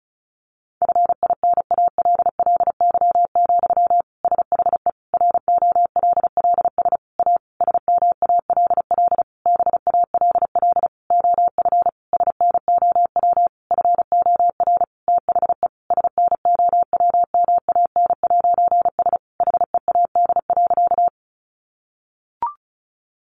Morse code, heard as a keyed tone, about 34 wpm, 700 Hz. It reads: FINALLY, SHE ROLLS A SMALL BALL OF SNOW FOR THE SNOWMAN'S HEAD.
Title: Easy Phrases 35wpm